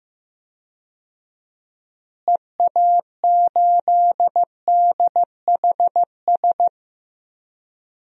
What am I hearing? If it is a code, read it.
Morse code, heard as a keyed tone, about 15 wpm, 700 Hz. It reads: EA8DHS
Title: DXCC-100 Call Signs - 6 Characters - Repeated 2x - 15wpm